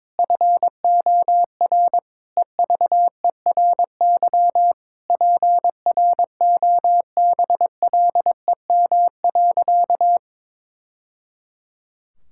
Morse code, heard as a keyed tone, about 22 wpm, 700 Hz. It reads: FOR EVERY PROBLEM.
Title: Sentences from Top 1000 Words - Review Part 5 - 22wpm